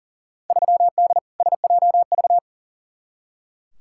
Morse code, heard as a keyed tone, about 40 wpm, 700 Hz. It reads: 3D SJV
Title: Field Day Exchanges - 40wpm